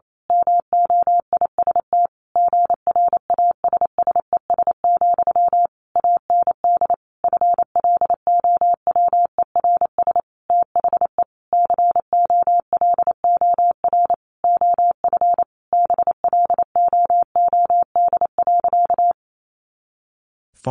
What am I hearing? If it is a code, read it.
Morse code, heard as a keyed tone, about 28 wpm, 700 Hz. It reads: MOIST GRASSES, AND FLOWERS THE COLOR OF BLOOD.
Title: A Sound Of Thunder 28wpm